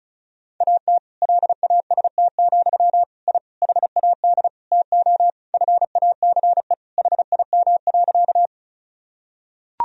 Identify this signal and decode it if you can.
Morse code, heard as a keyed tone, about 36 wpm, 700 Hz. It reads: AT LAST, I HAD TO FACE HIM.
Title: Sentences from Top 1000 Words - Review Part 2 - 35wpm